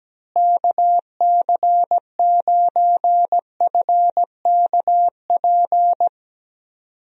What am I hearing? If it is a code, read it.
Morse code, heard as a keyed tone, about 17 wpm, 700 Hz. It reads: KC9FKP
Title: USA Call Signs - 6 Characters - Repeated 2x - 17wpm